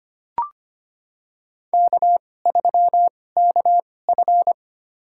Morse code, heard as a keyed tone, about 25 wpm, spaced as for 18 wpm, 700 Hz. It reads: K3KF
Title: USA Call Signs - 4 Characters - 2x Character Spacing - 25wpm